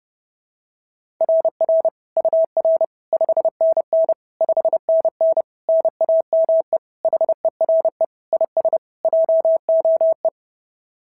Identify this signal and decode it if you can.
Morse code, heard as a keyed tone, about 26 wpm, 650 Hz. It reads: RR UR 5NN 5NN NAME HERE IS JOE
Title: SOTA QSO Practice 30wpm